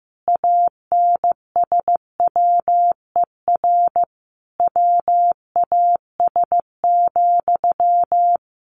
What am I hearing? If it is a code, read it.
Morse code, heard as a keyed tone, about 15 wpm, 700 Hz. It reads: ANSWER WAS,
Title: Sentences from Top 1100 Words 15wpm